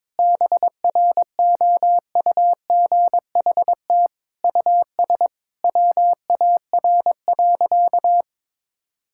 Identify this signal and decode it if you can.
Morse code, heard as a keyed tone, about 21 wpm, 700 Hz. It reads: BROUGHT US WAR.
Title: Sentences from Top 1000 Words - Review Part 6 - 22wpm